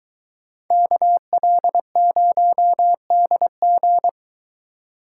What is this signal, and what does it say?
Morse code, heard as a keyed tone, about 23 wpm, 700 Hz. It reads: KL0DG